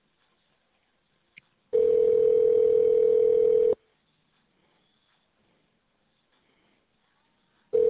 Ringback tone (North America), 440 + 480 Hz.